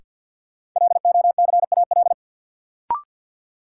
Morse code, heard as a keyed tone, about 53 wpm, 700 Hz. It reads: LOCAL
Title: Sentences from Top 1500 Words - New Word Drill Alt - 50wpm